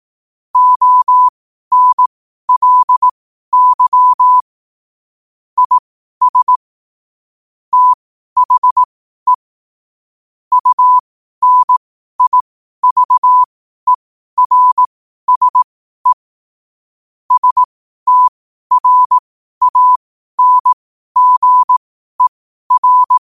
Morse code, heard as a keyed tone, about 18 wpm, spaced as for 12 wpm, 1000 Hz. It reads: ONLY IS THE UNIVERSE STRANGER
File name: qotdfiles / 12wpm